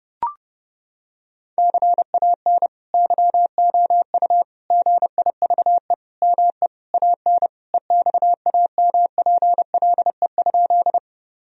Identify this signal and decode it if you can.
Morse code, heard as a keyed tone, about 30 wpm, 700 Hz. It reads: CAN YOU GIVE ME AN EXAMPLE?
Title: Sentences from Top 1000 Words - Review Part 1 - 30wpm